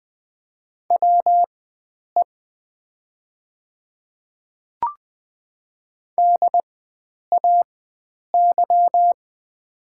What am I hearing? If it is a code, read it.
Morse code, heard as a keyed tone, about 20 wpm, spaced as for 9 wpm, 690 Hz. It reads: WE DAY
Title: 4x Character Spacing - Top 100 Words - 20wpm